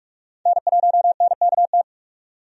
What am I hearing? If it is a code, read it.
Morse code, heard as a keyed tone, about 46 wpm, 700 Hz. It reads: N1NKT
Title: USA Call Signs - 5 Characters - 45wpm